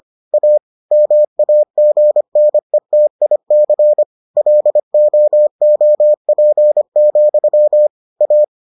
Morse code, heard as a keyed tone, about 25 wpm, 600 Hz. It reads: A MAGNETIC LOOP, A